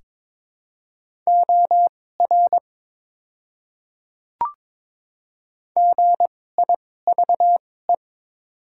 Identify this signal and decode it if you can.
Morse code, heard as a keyed tone, about 22 wpm, spaced as for 16 wpm, 700 Hz. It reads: OR GIVE
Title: 2x Character Spacing - Top 200 Words - 22wpm